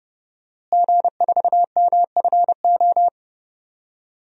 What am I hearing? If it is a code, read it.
Morse code, heard as a keyed tone, about 29 wpm, 700 Hz. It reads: G4MFO
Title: DXCC-100 Call Signs - 5 Characters - Repeated 2x - 30wpm